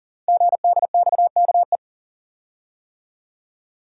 Morse code, heard as a keyed tone, about 38 wpm, 690 Hz. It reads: GDXKE
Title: Code Groups - 5 Letters - 40wpm